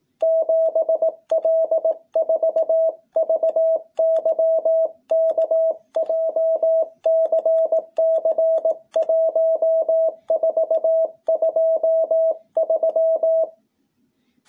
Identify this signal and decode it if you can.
Morse code, heard as a keyed tone, about 18 wpm, 660 Hz. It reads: 7L4VYK JCC 1423
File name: jh8cbh / 2021cw